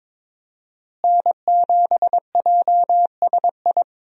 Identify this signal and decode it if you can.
Morse code, heard as a keyed tone, about 22 wpm, 700 Hz. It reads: N7JSI
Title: USA Call Signs - 5 Characters - Repeated 2x - 22wpm